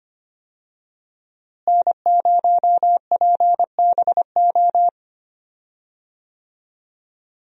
Morse code, heard as a keyed tone, about 25 wpm, 700 Hz. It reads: N0PBO
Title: USA Call Signs - 5 Characters - Repeated 2x - 25wpm